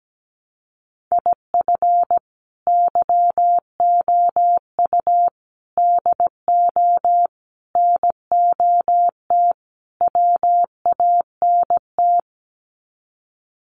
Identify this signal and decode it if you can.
Morse code, heard as a keyed tone, about 17 wpm, 700 Hz. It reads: IF YOU DO NOT WANT
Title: Sets of 5 Words 17wpm